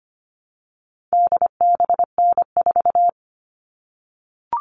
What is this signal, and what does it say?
Morse code, heard as a keyed tone, about 25 wpm, 700 Hz. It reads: DBN4
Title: All Cut Numbers - 4-Digit 25wpm